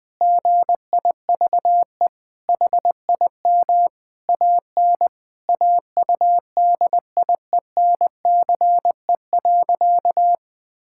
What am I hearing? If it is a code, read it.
Morse code, heard as a keyed tone, about 20 wpm, 700 Hz. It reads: GIVE HIM AN AUDIENCE.
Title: Sentences from Top 1000 Words - Review Part 5 - 20wpm